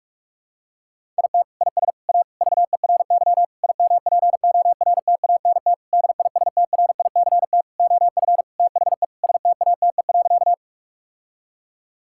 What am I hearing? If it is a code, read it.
Morse code, heard as a keyed tone, about 46 wpm, 700 Hz. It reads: IT IS A VERY IMPORTANT DISTRICT OF THE STATE.
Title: Sentences from Top 1600 Words 45wpm